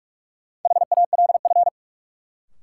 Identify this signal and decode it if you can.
Morse code, heard as a keyed tone, about 46 wpm, 700 Hz. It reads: HALF
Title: Sets of 3 Words - New Word Drill - 45wpm